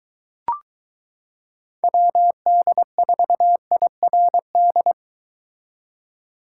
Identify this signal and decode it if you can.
Morse code, heard as a keyed tone, about 23 wpm, 700 Hz. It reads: WD4IRD